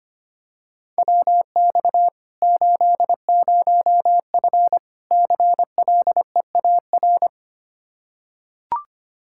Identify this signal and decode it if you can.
Morse code, heard as a keyed tone, about 25 wpm, 700 Hz. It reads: WX 80F CLEAR
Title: Practice QSO - Exchange 2 - Info - 25wpm